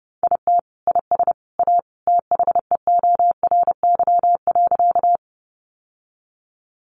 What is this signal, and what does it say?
Morse code, heard as a keyed tone, about 30 wpm, 700 Hz. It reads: IT IS A THEORY.
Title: Sentences from Top 600 Words 30wpm